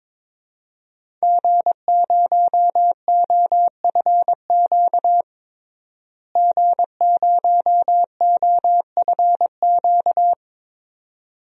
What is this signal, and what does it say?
Morse code, heard as a keyed tone, about 22 wpm, 700 Hz. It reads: G0OFQ G0OFQ
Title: DXCC-100 Call Signs - 5 Characters - Repeated 2x - 22wpm